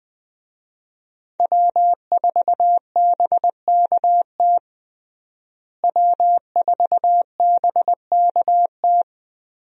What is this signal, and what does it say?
Morse code, heard as a keyed tone, about 21 wpm, 700 Hz. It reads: W4BKT W4BKT
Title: USA Call Signs - 5 Characters - Repeated 2x - 20wpm